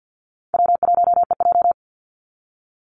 Morse code, heard as a keyed tone, about 50 wpm, 700 Hz. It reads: W1EJ
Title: NA SOTA Activator Callsigns 50wpm